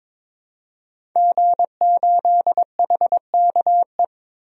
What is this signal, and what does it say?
Morse code, heard as a keyed tone, about 22 wpm, 700 Hz. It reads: G8HKE
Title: DXCC-100 Call Signs - 5 Characters - Repeated 3x - 22wpm